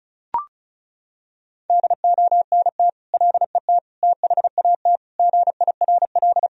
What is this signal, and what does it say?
Morse code, heard as a keyed tone, about 34 wpm, 700 Hz. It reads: DONT LET THAT GIRL
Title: Sentences from Top 1000 Words - Review Part 1 - 35wpm